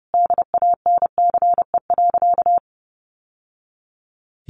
Morse code, heard as a keyed tone, about 30 wpm, 700 Hz. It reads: DANCE.
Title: Sentences from Top 700 Words 30wpm